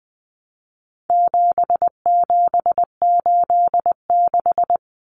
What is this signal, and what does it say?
Morse code, heard as a keyed tone, about 20 wpm, 700 Hz. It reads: 7786
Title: Numbers - 4-Digit 20wpm